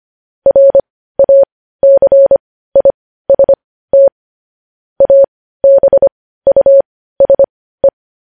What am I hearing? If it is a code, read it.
Morse code, heard as a keyed tone, about 25 wpm, spaced as for 15 wpm, 550 Hz. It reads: RACIST ABUSE
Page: News Headlines in Morse Code at 15 WPM